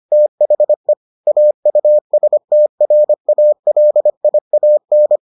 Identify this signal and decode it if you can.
Morse code, heard as a keyed tone, about 25 wpm, 600 Hz. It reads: THE AUSTRALIAN